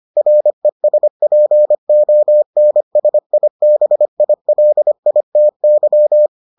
Morse code, heard as a keyed tone, about 25 wpm, 600 Hz. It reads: RESPONSIBILITY